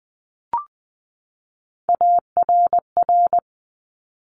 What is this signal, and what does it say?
Morse code, heard as a keyed tone, about 20 wpm, 700 Hz. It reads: ARR
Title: Top Three Letter Combinations in Words 20wpm